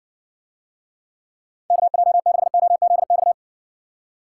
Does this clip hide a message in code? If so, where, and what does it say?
Yes: Morse code from 1.7 to 3.3 s, heard as a keyed tone, about 60 wpm, 700 Hz. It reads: KJ6OZX